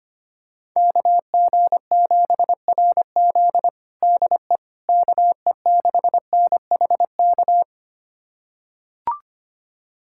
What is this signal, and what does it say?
Morse code, heard as a keyed tone, about 25 wpm, 700 Hz. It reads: KG7RZ DE KE6NHK
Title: USA Callsign DE Callsign 25wpm